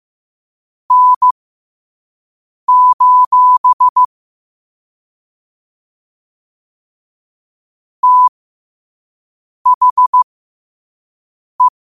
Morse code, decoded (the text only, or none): N: THE